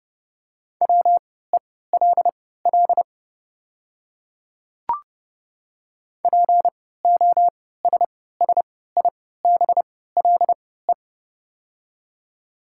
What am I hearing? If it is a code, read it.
Morse code, heard as a keyed tone, about 31 wpm, spaced as for 17 wpm, 700 Hz. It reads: WELL POSSIBLE
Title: 3x Character Spacing - Top 500 Words - 30wpm